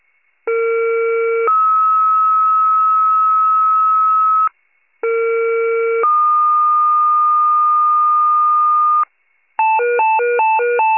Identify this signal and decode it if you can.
Two-tone page, heard as a one-second tone, then a three-second tone, about 460 then 1290 Hz, about 460 then 1220 Hz. It is heard from 0.5 to 9.0 s.